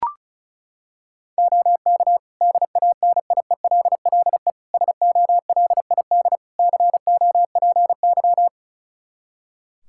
Morse code, heard as a keyed tone, about 34 wpm, 700 Hz. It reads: OK DANIELLE SOLID COPY
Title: Practice QSO - Exchange 3 - More Info - 35wpm